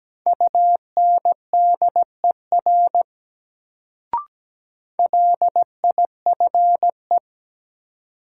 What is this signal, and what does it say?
Morse code, heard as a keyed tone, about 17 wpm, 700 Hz. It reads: UNDER LIFE
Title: Sentences from Top 300 Words - New Word Drill - 17wpm